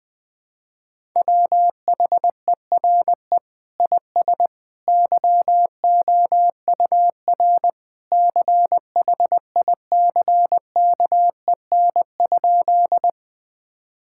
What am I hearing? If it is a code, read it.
Morse code, heard as a keyed tone, about 20 wpm, 700 Hz. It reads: WHERE IS YOUR CHICKEN?
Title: Sentences from Top 1000 Words - Review Part 5 - 20wpm